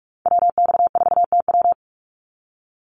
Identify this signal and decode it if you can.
Morse code, heard as a keyed tone, about 44 wpm, 700 Hz. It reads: WX4TW
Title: NA SOTA Activator Callsigns 45wpm